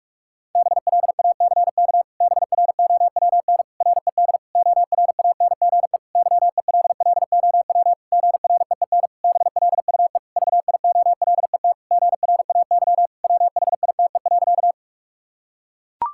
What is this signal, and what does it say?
Morse code, heard as a keyed tone, about 43 wpm, 700 Hz. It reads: BLACK BROWN RED ORANGE YELLOW GREEN BLUE VIOLET GRAY WHITE.
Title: Pangram Challenge v2 - 45wpm